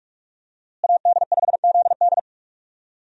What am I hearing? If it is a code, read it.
Morse code, heard as a keyed tone, about 41 wpm, 700 Hz. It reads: AD5ZD
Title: USA Call Signs - 5 Characters - Repeated 3x - 45wpm